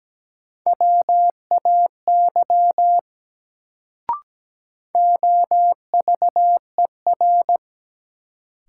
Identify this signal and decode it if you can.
Morse code, heard as a keyed tone, about 17 wpm, 700 Hz. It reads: WAY OVER
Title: Sentences from Top 100 Words - New Word Drill - 17wpm